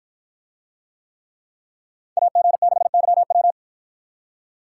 Morse code, heard as a keyed tone, about 53 wpm, 700 Hz. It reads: AG6YW